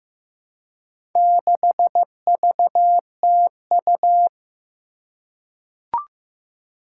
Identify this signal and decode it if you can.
Morse code, heard as a keyed tone, about 15 wpm, 700 Hz. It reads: 6VTU
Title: Four Challenging Letter-Number Combinations 15wpm